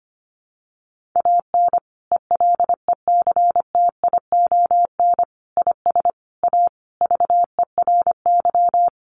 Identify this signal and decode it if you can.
Morse code, heard as a keyed tone, about 25 wpm, 700 Hz. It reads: AN ELECTION IS A VERY
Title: Sentences from Top 1500 Words 25wpm